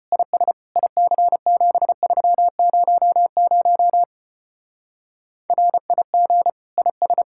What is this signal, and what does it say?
Morse code, heard as a keyed tone, about 29 wpm, 700 Hz. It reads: IS IC7300 RIG IS
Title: Speed-Racing Common QSO Phrases Encore - 25wpm